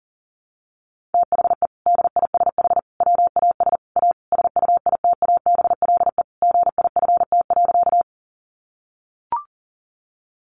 Morse code, heard as a keyed tone, about 40 wpm, 700 Hz. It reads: THE DISH WAS A SUITABLE GIFT.
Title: Sentences from Top 1400 Words 40wpm